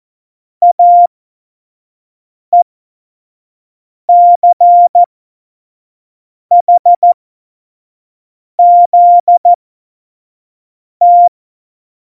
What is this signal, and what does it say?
Morse code, heard as a keyed tone, about 14 wpm, spaced as for 5 wpm, 700 Hz. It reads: AECHZT
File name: Morsekurs / Witze / Witz 17.Juli 2018, 25 F70